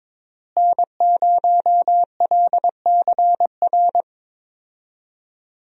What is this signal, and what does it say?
Morse code, heard as a keyed tone, about 21 wpm, 700 Hz. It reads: N0LCR
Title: USA Call Signs - 5 Characters - Repeated 2x - 22wpm